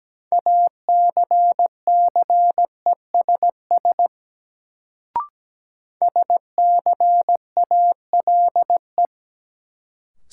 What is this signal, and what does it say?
Morse code, heard as a keyed tone, about 17 wpm, 700 Hz. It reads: ACCESS SCALE
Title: Sentences from Top 1000 Words - Review Part 4 - New Word Drill - 17wpm